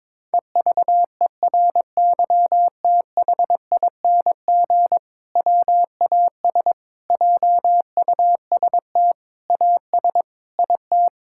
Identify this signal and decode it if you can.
Morse code, heard as a keyed tone, about 22 wpm, 700 Hz. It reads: EVERYTHING WAS JUST AS IT
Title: Sentences from Top 1600 Words 22wpm